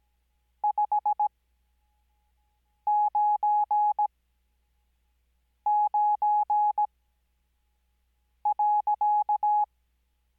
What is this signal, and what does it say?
Morse code, heard as a keyed tone, about 17 wpm, spaced as for 5 wpm, 840 Hz. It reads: 599.
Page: Farnsworth Spacing: 15WPM Character speed Spaced at 5WPM.